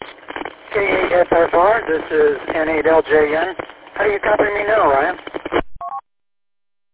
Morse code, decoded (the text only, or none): none